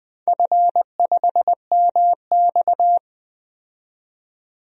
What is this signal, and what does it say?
Morse code, heard as a keyed tone, about 20 wpm, 700 Hz. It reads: F5MX
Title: DXCC-100 Call Signs - 4 Characters - 20wpm